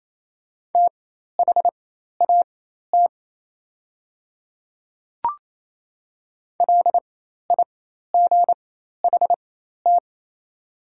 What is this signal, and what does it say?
Morse code, heard as a keyed tone, about 28 wpm, spaced as for 13 wpm, 700 Hz. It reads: THAT LIGHT
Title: 4x Character Spacing - Top 300 Words - 28wpm